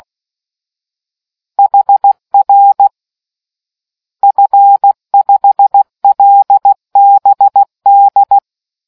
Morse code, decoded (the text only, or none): HR F5LBD